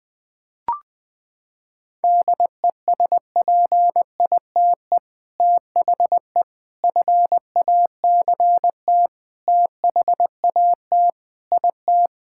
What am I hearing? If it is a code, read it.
Morse code, heard as a keyed tone, about 20 wpm, 700 Hz. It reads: DESPITE THE FACT THAT IT
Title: Sentences with 4-Word Phrases - 20wpm